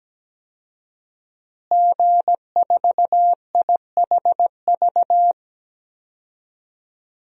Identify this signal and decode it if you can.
Morse code, heard as a keyed tone, about 17 wpm, 700 Hz. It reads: G4IHV
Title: DXCC-100 Call Signs - 5 Characters - Repeated 2x - 17wpm